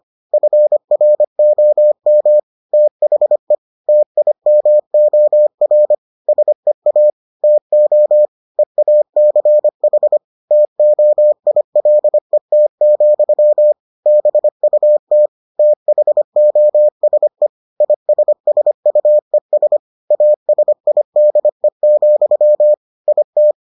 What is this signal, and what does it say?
Morse code, heard as a keyed tone, about 25 wpm, 600 Hz. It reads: FROM THE TIMOR SEA TO EACH TOILET, BUT THOSE ISSUES ASIDE, IT